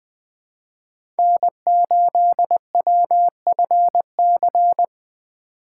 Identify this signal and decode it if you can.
Morse code, heard as a keyed tone, about 20 wpm, 700 Hz. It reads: N8WFC
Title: USA Call Signs - 5 Characters - 2x Character Spacing - 20wpm